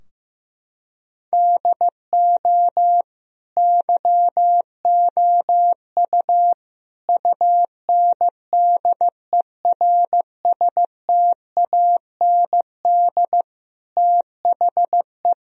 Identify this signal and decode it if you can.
Morse code, heard as a keyed tone, about 15 wpm, 700 Hz. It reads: DO YOU UNDERSTAND THE
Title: Sentences from Top 1000 Words - Review Part 3 - 15wpm